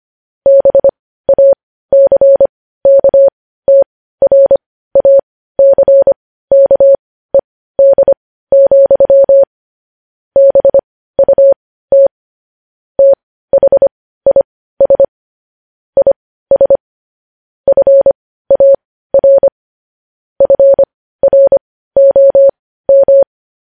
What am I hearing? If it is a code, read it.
Morse code, heard as a keyed tone, about 25 wpm, spaced as for 15 wpm, 550 Hz. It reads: BACKTRACKED, BUT THIS IS FAR FROM
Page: news-headlines-in-morse-code-at-15-wpm